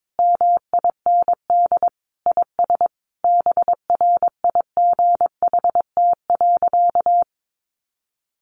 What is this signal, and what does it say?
Morse code, heard as a keyed tone, about 22 wpm, 700 Hz. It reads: MIND IS BRIGHT.
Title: Sentences from Top 1500 Words 22wpm